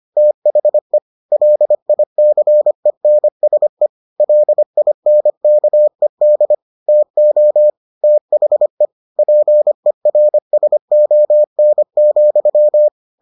Morse code, heard as a keyed tone, about 25 wpm, 600 Hz. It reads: THE LICENSE LINKED TO THE PERSON,